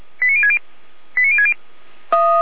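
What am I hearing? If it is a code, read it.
Five-tone selective call (ZVEI), heard as quick runs of five tones, tones of about 70 ms. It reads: 89060 89060
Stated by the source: Five-tone ZVEI selective call received on SDR.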